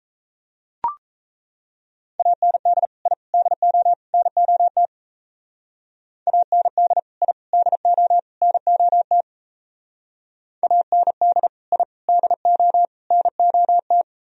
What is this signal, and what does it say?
Morse code, heard as a keyed tone, about 34 wpm, 700 Hz. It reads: AND I DO NOT AND I DO NOT AND I DO NOT
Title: Speed-Racing - Sets of 4 Words 28wpm